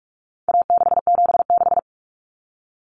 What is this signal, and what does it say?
Morse code, heard as a keyed tone, about 44 wpm, 700 Hz. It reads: A676
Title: Common Cut Numbers - 4-Digit 45wpm